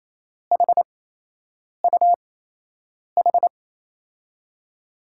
Morse code, heard as a keyed tone, about 27 wpm, spaced as for 8 wpm, 700 Hz. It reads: HUH
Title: USA Call Signs - 6 Characters - 8x Character Spacing - 28wpm